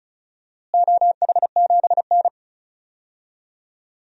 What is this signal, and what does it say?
Morse code, heard as a keyed tone, about 34 wpm, 700 Hz. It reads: OH7N
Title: DXCC-100 Call Signs - 4 Characters - Repeated 3x - 35wpm